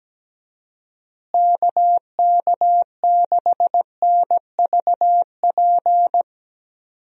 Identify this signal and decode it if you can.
Morse code, heard as a keyed tone, about 17 wpm, 700 Hz. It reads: KK6NVP
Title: USA Call Signs - 6 Characters - Repeated 3x - 17wpm